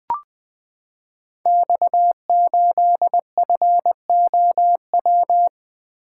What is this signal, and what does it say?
Morse code, heard as a keyed tone, about 20 wpm, 700 Hz. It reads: X8FOW
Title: Code Groups - 5 Letter:Numbers - 20wpm